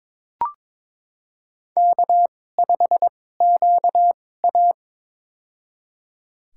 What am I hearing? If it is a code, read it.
Morse code, heard as a keyed tone, about 22 wpm, spaced as for 16 wpm, 700 Hz. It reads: K5QA
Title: USA Call Signs - 4 Characters - 2x Character Spacing - 22wpm